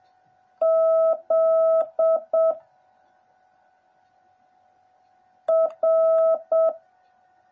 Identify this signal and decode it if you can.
Morse code, heard as a keyed tone, about 7 wpm, 650 Hz. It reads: Z R
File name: jh8cbh / 2021cw / cw09a